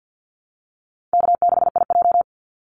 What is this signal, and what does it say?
Morse code, heard as a keyed tone, about 50 wpm, 700 Hz. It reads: K6IJ
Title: Easy Call Signs Repeated 2x - 50wpm